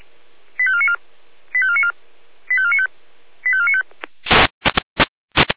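Five-tone selective call (ZVEI), heard as quick runs of five tones, tones of about 70 ms. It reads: 85494 85494 85495 85495
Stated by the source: ZVEI five-tone selective call received by SDR.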